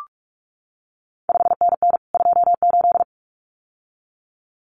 Morse code, heard as a keyed tone, about 46 wpm, 700 Hz. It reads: 5NN 28
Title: CQ WW DX Contest Exchange Encore - 45wpm